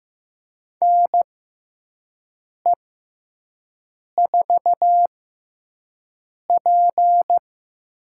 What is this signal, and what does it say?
Morse code, heard as a keyed tone, about 15 wpm, spaced as for 5 wpm, 700 Hz. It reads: NE4P
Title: USA Call Signs - 4 Characters - 6x Character Spacing - 15wpm